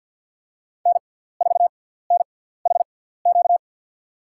Morse code, heard as a keyed tone, about 51 wpm, spaced as for 17 wpm, 700 Hz. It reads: N4NHQ
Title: USA Call Signs - 5 Characters - 6x Character Spacing - 50wpm